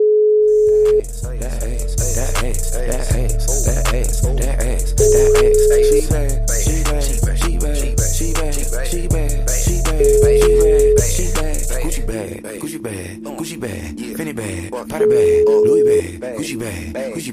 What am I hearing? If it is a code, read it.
Ringback tone (Europe, 425 Hz), with speech.